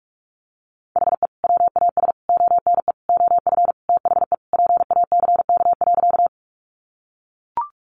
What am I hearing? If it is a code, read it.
Morse code, heard as a keyed tone, about 44 wpm, 700 Hz. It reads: HE WAS ONE OF THE PACK.
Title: Sentences from Top 1300 Words 45wpm